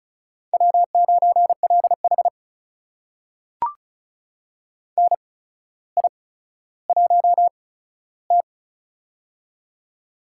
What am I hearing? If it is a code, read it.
Morse code, heard as a keyed tone, about 36 wpm, spaced as for 15 wpm, 700 Hz. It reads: W9LH NI1T
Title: USA Call Signs - 4 Characters - 8x Character Spacing - 35wpm